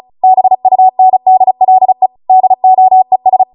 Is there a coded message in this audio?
Yes: Morse code from 0.2 to 3.4 s, heard as a keyed tone, about 36 wpm, 750 Hz. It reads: BUNDLE DOES